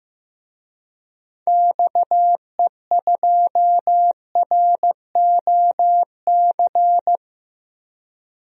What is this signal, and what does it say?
Morse code, heard as a keyed tone, about 15 wpm, 700 Hz. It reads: XE2ROC
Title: DXCC-100 Call Signs - 6 Characters - Repeated 2x - 15wpm